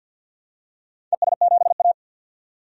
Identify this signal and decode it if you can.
Morse code, heard as a keyed tone, about 51 wpm, 700 Hz. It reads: ES7A
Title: DXCC-100 Call Signs - 4 Characters - 50wpm